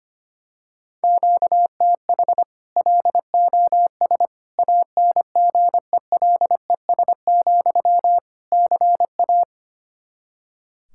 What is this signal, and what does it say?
Morse code, heard as a keyed tone, about 25 wpm, 700 Hz. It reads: QTH LOS ANGELES, CA
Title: Practice QSO - Exchange 1 - Beginning - 25wpm